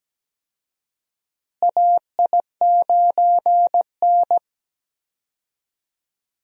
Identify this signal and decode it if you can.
Morse code, heard as a keyed tone, about 17 wpm, 700 Hz. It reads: AI9N
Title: USA Call Signs - 4 Characters - Repeated 3x - 17wpm